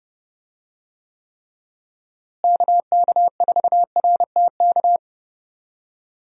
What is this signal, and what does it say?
Morse code, heard as a keyed tone, about 31 wpm, 700 Hz. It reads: KK4RTK
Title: USA Call Signs - 6 Characters - Repeated 2x - 30wpm